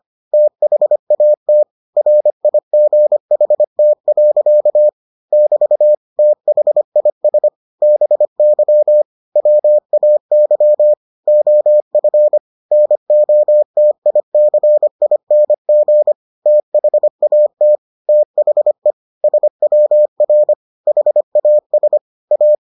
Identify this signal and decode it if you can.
Morse code, heard as a keyed tone, about 25 wpm, 600 Hz. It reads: THAT RIGHT. <BT> THIS BY WAY OF NOTICING THAT THE SWR HAS A